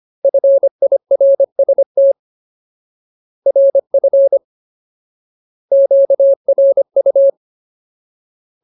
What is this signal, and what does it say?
Morse code, heard as a keyed tone, about 25 wpm, 550 Hz. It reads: FIRST RF QRU
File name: Ultimate Word List / 25WPM Ultimate